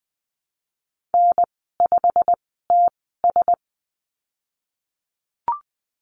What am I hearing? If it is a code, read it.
Morse code, heard as a keyed tone, about 20 wpm, spaced as for 15 wpm, 700 Hz. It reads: N5TS
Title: 088 Lesson 15g - Number 5 - Call Signs 20wpm